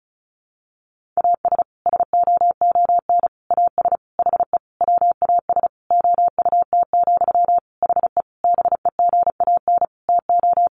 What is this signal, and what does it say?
Morse code, heard as a keyed tone, about 34 wpm, 700 Hz. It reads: AS SOON AS HE WAS OUT, HE BEGAN TO
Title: Sentences from Top 1500 Words 35wpm